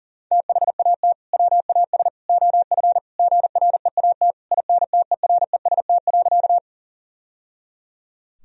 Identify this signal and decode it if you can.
Morse code, heard as a keyed tone, about 40 wpm, 700 Hz. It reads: THAT WAS OF GREAT INTEREST.
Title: Sentences from Top 1000 Words - Review Part 2 - 40wpm